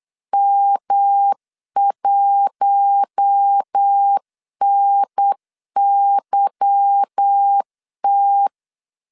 Morse code, heard as a keyed tone, about 8 wpm, 800 Hz. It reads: M1NYT
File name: LEZIONI / 9 2 8wpm 3437